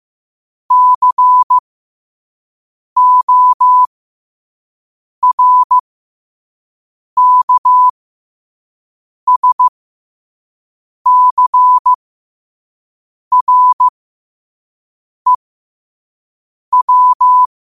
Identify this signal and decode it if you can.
Morse code, heard as a keyed tone, about 15 wpm, spaced as for 5 wpm, 1000 Hz. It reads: CORKSCREW
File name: qotdfiles / 5wpm